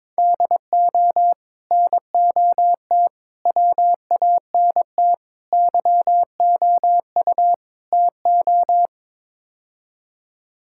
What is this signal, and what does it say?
Morse code, heard as a keyed tone, about 22 wpm, 700 Hz. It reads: DO NOT WANT YOU TO
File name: practice / 22wpm / Build A Phrase - 5-Word Phrases - 22wpm